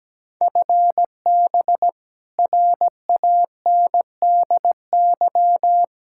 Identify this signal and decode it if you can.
Morse code, heard as a keyed tone, about 17 wpm, 700 Hz. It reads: FB RANDY
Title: Practice QSO - Exchange 3 - More Info - 17wpm